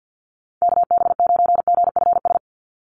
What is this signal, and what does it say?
Morse code, heard as a keyed tone, about 50 wpm, 700 Hz. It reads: KB9GFS
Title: USA Call Signs 50wpm